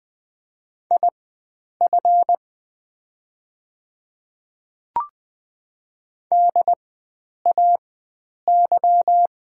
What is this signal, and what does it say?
Morse code, heard as a keyed tone, about 20 wpm, spaced as for 9 wpm, 700 Hz. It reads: IF DAY
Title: 4x Character Spacing - Top 100 Words - 20wpm